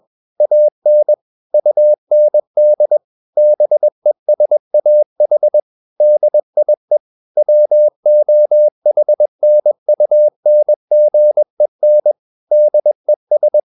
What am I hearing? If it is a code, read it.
Morse code, heard as a keyed tone, about 21 wpm, 600 Hz. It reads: AN UND BESAH DIE WOHNUNGEN DES